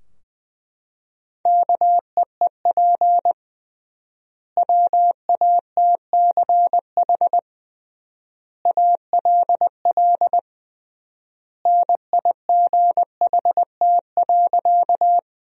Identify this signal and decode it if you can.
Morse code, heard as a keyed tone, about 21 wpm, 700 Hz. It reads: KEEP WATCH ALL NIGHT.
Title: 3x Word-Spacing - Sentences from Top 300 Words - 20wpm